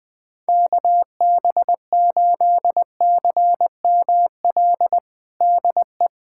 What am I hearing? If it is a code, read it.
Morse code, heard as a keyed tone, about 20 wpm, 700 Hz. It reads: KB8CML DE
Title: USA Callsign DE Callsign 20wpm